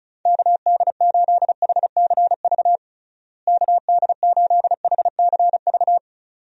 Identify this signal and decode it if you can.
Morse code, heard as a keyed tone, about 34 wpm, 700 Hz. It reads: KD8HCV KD8HCV
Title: USA Call Signs - 6 Characters - Repeated 2x - 35wpm